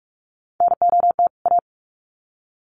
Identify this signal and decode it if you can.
Morse code, heard as a keyed tone, about 44 wpm, 700 Hz. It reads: NOT A
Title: Sets of 2 Words 45wpm